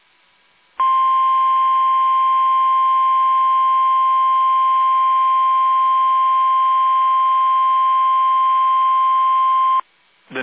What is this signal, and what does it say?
Emergency Alert System (SAME) tones (the 1050 Hz attention tone). It is heard from 0.7 to 9.9 s.
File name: eas / KUSO / KUSO